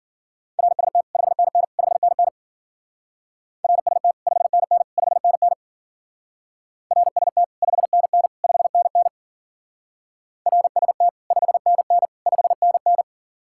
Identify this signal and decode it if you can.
Morse code, heard as a keyed tone, about 47 wpm, 700 Hz. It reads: RST 5NN 5NN RST 5NN 5NN RST 5NN 5NN RST 5NN 5NN
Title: Speed-Racing Common QSO Phrases - 40wpm